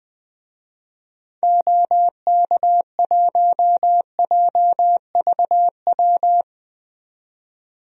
Morse code, heard as a keyed tone, about 20 wpm, 700 Hz. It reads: OK1JVW
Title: DXCC-100 Call Signs - 6 Characters - 20wpm